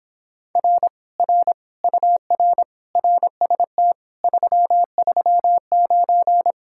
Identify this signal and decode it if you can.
Morse code, heard as a keyed tone, about 26 wpm, 700 Hz. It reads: R R UR RST 339